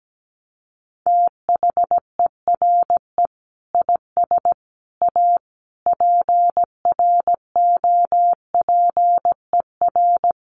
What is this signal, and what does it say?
Morse code, heard as a keyed tone, about 17 wpm, 700 Hz. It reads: THERE IS A PROPER
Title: Sentences from Top 800 Words 17wpm